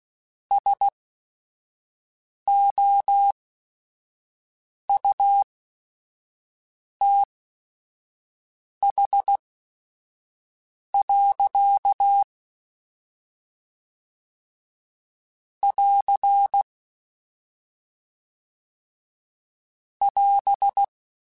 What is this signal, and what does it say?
Morse code, heard as a keyed tone, about 16 wpm, spaced as for 5 wpm, 800 Hz. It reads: SOUTH. <AR> <AS>
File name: Weather Sets / Weather Sets-mp3s / Weather 4 21.@5